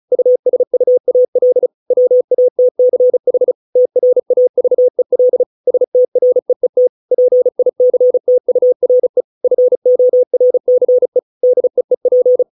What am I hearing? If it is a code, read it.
Morse code, heard as a keyed tone, about 34 wpm, 500 Hz. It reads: USUAL WATCH TRAVEL STREET PICTURE FORCE DEEP